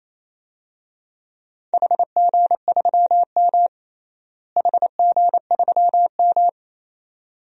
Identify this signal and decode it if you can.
Morse code, heard as a keyed tone, about 28 wpm, 700 Hz. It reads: HG3M HG3M
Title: DXCC-100 Call Signs - 4 Characters - Repeated 2x - 28wpm